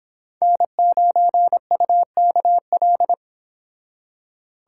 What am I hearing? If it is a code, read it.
Morse code, heard as a keyed tone, about 26 wpm, 700 Hz. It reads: N9UKL